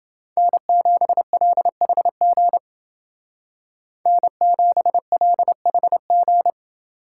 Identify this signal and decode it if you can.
Morse code, heard as a keyed tone, about 29 wpm, 700 Hz. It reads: N7LHG N7LHG